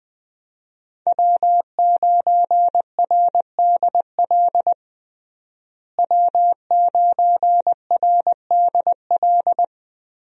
Morse code, heard as a keyed tone, about 20 wpm, 700 Hz. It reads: W9RDL W9RDL
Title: USA Call Signs - 5 Characters - Repeated 2x - 20wpm